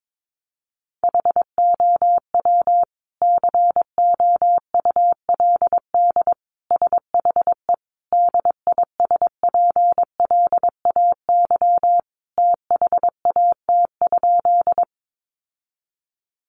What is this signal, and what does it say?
Morse code, heard as a keyed tone, about 22 wpm, 700 Hz. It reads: HOW COULD SHE DISPLAY THAT?
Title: Sentences from Top 1200 Words 22wpm